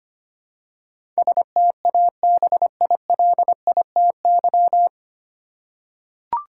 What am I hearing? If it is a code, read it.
Morse code, heard as a keyed tone, about 25 wpm, 690 Hz. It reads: STABILITY
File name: practice / 25wpm / 9 Letter Words - First 3 Letters Given - 25wpm